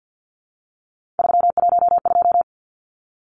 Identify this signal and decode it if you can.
Morse code, heard as a keyed tone, about 50 wpm, 700 Hz. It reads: 312
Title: Numbers - 3-Digit 50wpm